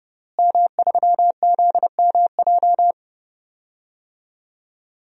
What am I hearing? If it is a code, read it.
Morse code, heard as a keyed tone, about 30 wpm, 700 Hz. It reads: M3ZMJ
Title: DXCC-100 Call Signs - 5 Characters - Repeated 3x - 30wpm